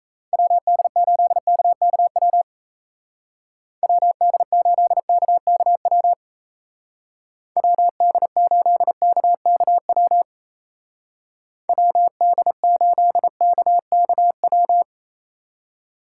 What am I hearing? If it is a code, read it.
Morse code, heard as a keyed tone, about 34 wpm, 700 Hz. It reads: WD8KKW WD8KKW WD8KKW WD8KKW
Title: Speed Racing - USA 2x3 Call Signs 28wpm